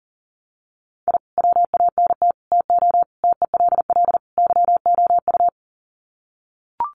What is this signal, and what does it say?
Morse code, heard as a keyed tone, about 40 wpm, 700 Hz. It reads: I WANT TO TELL YOU
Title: Sets of 5 Words Encore 40wpm